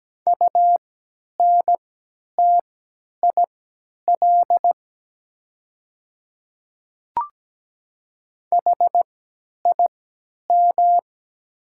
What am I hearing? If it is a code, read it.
Morse code, heard as a keyed tone, about 17 wpm, spaced as for 10 wpm, 700 Hz. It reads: UNTIL HIM
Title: 3x Character Spacing - Top 300 Words - 17wpm